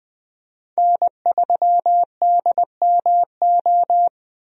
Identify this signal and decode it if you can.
Morse code, heard as a keyed tone, about 20 wpm, 700 Hz. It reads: N3DMO
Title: USA Call Signs - 5 Characters - Repeated 3x - 20wpm